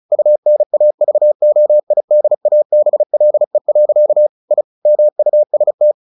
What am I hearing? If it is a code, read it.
Morse code, heard as a keyed tone, about 34 wpm, 600 Hz. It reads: UNAVOIDABLE. I MUST
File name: the old man and the sea / the old man and the sea 1 35wpm